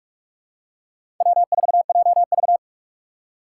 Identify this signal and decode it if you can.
Morse code, heard as a keyed tone, about 43 wpm, 700 Hz. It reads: W4JV
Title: USA Call Signs - 4 Characters - 3x Character Spacing - 45wpm